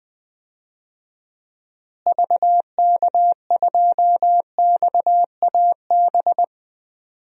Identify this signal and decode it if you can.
Morse code, heard as a keyed tone, about 20 wpm, 700 Hz. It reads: VK2XAB
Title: DXCC-100 Call Signs - 6 Characters - Repeated 2x - 20wpm